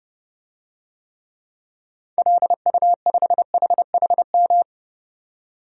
Morse code, heard as a keyed tone, about 30 wpm, 700 Hz. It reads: LU5HHM
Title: DXCC-100 Call Signs - 6 Characters - 30wpm